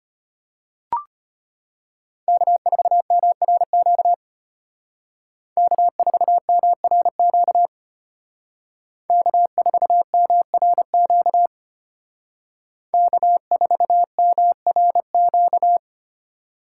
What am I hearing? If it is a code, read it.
Morse code, heard as a keyed tone, about 31 wpm, 700 Hz. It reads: K4MRQ K4MRQ K4MRQ K4MRQ
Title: Speed Racing - USA 1x3 Call Signs 25wpm